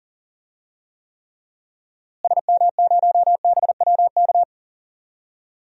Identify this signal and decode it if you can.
Morse code, heard as a keyed tone, about 39 wpm, 700 Hz. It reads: SM0BWK
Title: DXCC-100 Call Signs - 6 Characters - Repeated 3x - 40wpm